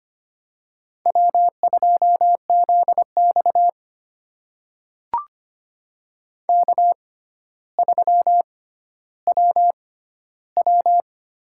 Morse code, heard as a keyed tone, about 25 wpm, spaced as for 13 wpm, 700 Hz. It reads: W2ZX K3WW
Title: USA Call Signs - 4 Characters - 6x Character Spacing - 25wpm